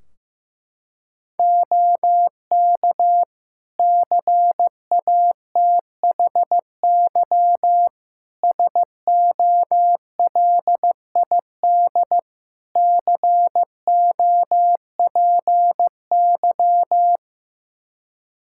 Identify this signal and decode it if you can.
Morse code, heard as a keyed tone, about 15 wpm, 700 Hz. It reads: OK CATHY SOLID COPY
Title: K1USN SST QSO Practice 15wpm